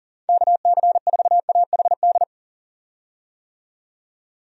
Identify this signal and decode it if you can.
Morse code, heard as a keyed tone, about 40 wpm, 700 Hz. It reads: KC4AHD
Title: USA Call Signs - 6 Characters - 40wpm